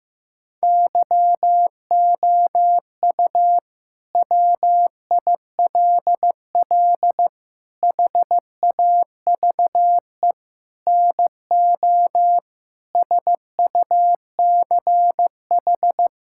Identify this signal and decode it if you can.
Morse code, heard as a keyed tone, about 16 wpm, 700 Hz. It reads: YOU WILL HAVE NO SUCH
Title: Sentences from Top 1000 Words - Review Part 6 - 15wpm